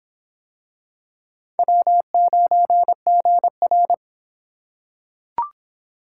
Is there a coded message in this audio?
Yes: Morse code from 1.6 to 3.9 s, heard as a keyed tone, about 25 wpm, 700 Hz. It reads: W9GR